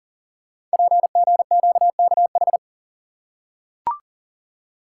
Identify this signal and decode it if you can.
Morse code, heard as a keyed tone, about 40 wpm, 700 Hz. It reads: PGQKH
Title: Code Groups - 5 Letters - 40wpm